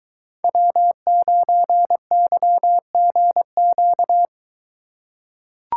Morse code, heard as a keyed tone, about 21 wpm, 700 Hz. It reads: W9YGQ